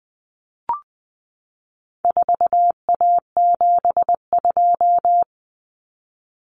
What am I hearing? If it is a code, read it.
Morse code, heard as a keyed tone, about 20 wpm, 700 Hz. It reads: 4A72
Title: Common Cut Numbers - 4-Digit 20wpm